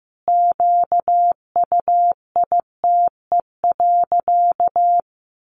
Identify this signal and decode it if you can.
Morse code, heard as a keyed tone, about 15 wpm, 700 Hz. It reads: QUITE.
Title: Sentences from Top 1300 Words 15wpm